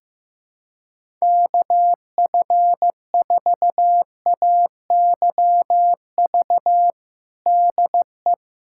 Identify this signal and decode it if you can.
Morse code, heard as a keyed tone, about 15 wpm, 700 Hz. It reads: KF4AYV DE
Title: USA Callsign DE Callsign - 6 Characters - 15wpm